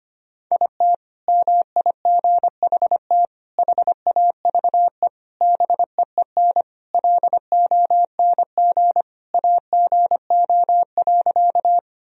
Morse code, heard as a keyed tone, about 25 wpm, 700 Hz. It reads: IT MIGHT HAVE BEEN LONG AGO.
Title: Sentences from Top 1000 Words - Review Part 3 - 25wpm